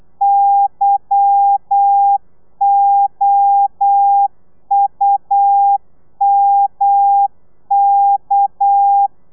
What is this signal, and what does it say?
Morse code, heard as a keyed tone, about 8 wpm, 800 Hz. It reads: YOUMK
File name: telegrafia / i2tfj / Lez11 35